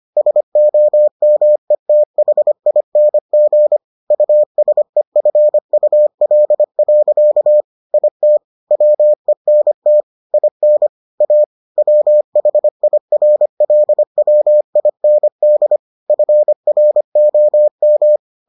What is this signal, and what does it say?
Morse code, heard as a keyed tone, about 25 wpm, 600 Hz. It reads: SOMETHING USEFUL. IT WENT IN A WHIRLWIND FROM